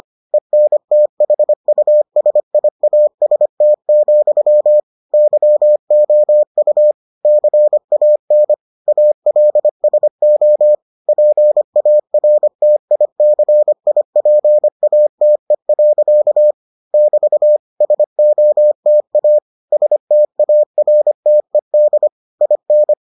Morse code, heard as a keyed tone, about 25 wpm, 600 Hz. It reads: ENTHUSIAST, YOU CAN ALSO PARTICIPATE. <BT> SOTA STARTED IN